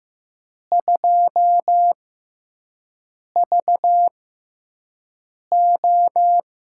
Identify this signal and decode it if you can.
Morse code, heard as a keyed tone, about 15 wpm, spaced as for 5 wpm, 700 Hz. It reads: 2VO
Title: USA Call Signs - 5 Characters - 6x Character Spacing - 15wpm